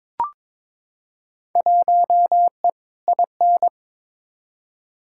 Morse code, heard as a keyed tone, about 23 wpm, 700 Hz. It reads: 1E IN
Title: Field Day Exchanges Encore - 22wpm